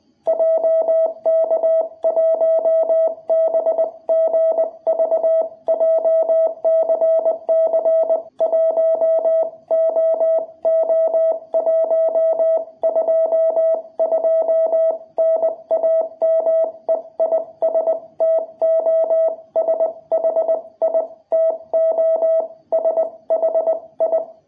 Morse code, heard as a keyed tone, about 20 wpm, spaced as for 17 wpm, 660 Hz. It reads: JK1BGV JCC1OO122 NAME IS TOSHI TOSHI
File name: jh8cbh / 2021cw